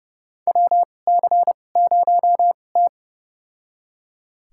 Morse code, heard as a keyed tone, about 29 wpm, spaced as for 21 wpm, 700 Hz. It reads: WC0T
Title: USA Call Signs - 4 Characters - 2x Character Spacing - 30wpm